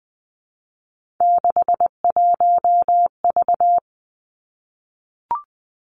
Morse code, heard as a keyed tone, about 20 wpm, 700 Hz. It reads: 61V
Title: Three Challenging Letter-Number Combinations 20wpm